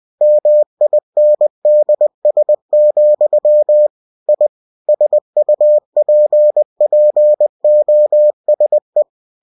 Morse code, heard as a keyed tone, about 20 wpm, 600 Hz. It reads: MINDS, I SUPPOSE
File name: cw-books / winnie-the-pooh / winnie-20wpm-page-0088